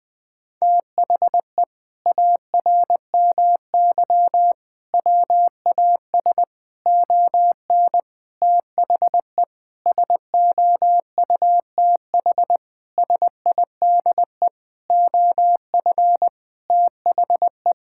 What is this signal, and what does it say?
Morse code, heard as a keyed tone, about 20 wpm, 700 Hz. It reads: THE ARMY WAS ON THE SOUTH SIDE OF THE
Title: Sentences from Top 1000 Words - Review Part 5 - 20wpm